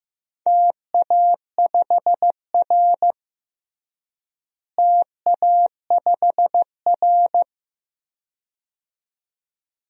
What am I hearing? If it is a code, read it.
Morse code, heard as a keyed tone, about 16 wpm, 700 Hz. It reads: TA5R TA5R
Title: DXCC-100 Call Signs - 4 Characters - Repeated 2x - 15wpm